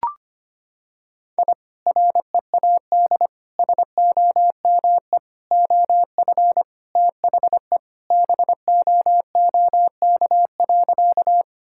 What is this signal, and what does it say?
Morse code, heard as a keyed tone, about 25 wpm, 700 Hz. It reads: I READ SOME OF THE BOOK.
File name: practice / 25wpm / Sentences with 2-Word Phrases - 25wpm